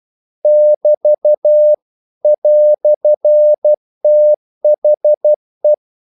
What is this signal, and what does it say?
Morse code, heard as a keyed tone, about 12 wpm, 600 Hz. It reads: <BT> "THE
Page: English quotations in CW (telegraphy - morse)